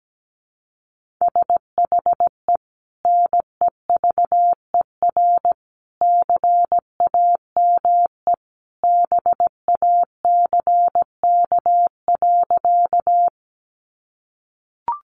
Morse code, heard as a keyed tone, about 17 wpm, 700 Hz. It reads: SHE NEVER CAME BACK.
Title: Sentences from Top 300 Words 17wpm